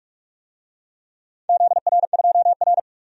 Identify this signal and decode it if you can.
Morse code, heard as a keyed tone, about 47 wpm, 700 Hz. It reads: ZR2R
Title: DXCC-100 Call Signs - 4 Characters - 45wpm